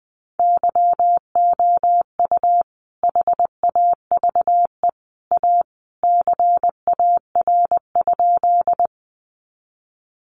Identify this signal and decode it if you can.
Morse code, heard as a keyed tone, about 20 wpm, 700 Hz. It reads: YOU HAVE A CAR?
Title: Sentences from Top 300 Words 20wpm